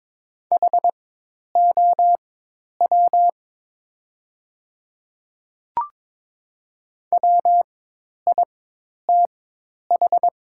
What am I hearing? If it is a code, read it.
Morse code, heard as a keyed tone, about 22 wpm, spaced as for 10 wpm, 700 Hz. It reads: HOW WITH
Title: 4x Character Spacing - Top 100 Words - 22wpm